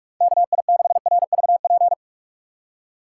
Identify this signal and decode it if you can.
Morse code, heard as a keyed tone, about 45 wpm, 700 Hz. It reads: KI6RVP
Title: USA Call Signs - 6 Characters - 4x Character Spacing - 45wpm